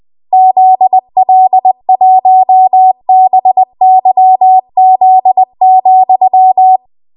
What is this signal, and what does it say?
Morse code, heard as a keyed tone, about 20 wpm, 750 Hz. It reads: ZL1BYZ,
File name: Archive / 20 WPM / 20WPM